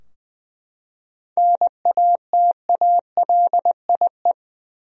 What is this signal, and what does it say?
Morse code, heard as a keyed tone, about 20 wpm, 700 Hz. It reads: NATALIE
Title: USA Common Names - 20wpm